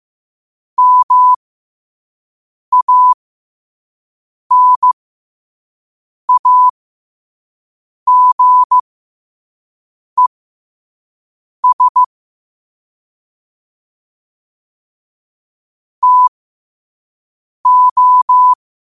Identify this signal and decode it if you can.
Morse code, heard as a keyed tone, about 15 wpm, spaced as for 5 wpm, 1000 Hz. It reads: MANAGES TO
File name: qotdfiles / 5wpm